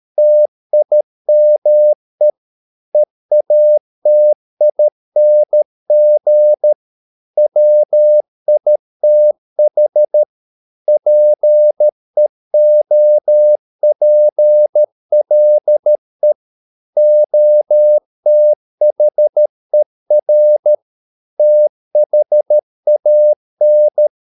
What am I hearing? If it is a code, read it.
Morse code, heard as a keyed tone, about 13 wpm, 600 Hz. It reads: TIME EATING WITH PEOPLE OTHER THAN